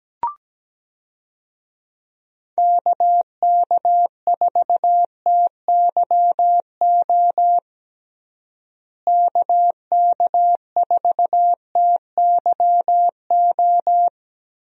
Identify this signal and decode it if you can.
Morse code, heard as a keyed tone, about 17 wpm, 700 Hz. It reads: KK4TYO KK4TYO
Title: USA Call Signs - 6 Characters - Repeated 3x - 17wpm